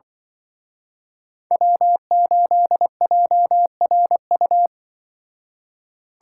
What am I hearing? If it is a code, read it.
Morse code, heard as a keyed tone, about 25 wpm, 700 Hz. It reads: W8JRU
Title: Speed Racing - USA 1x3 Call Signs 20wpm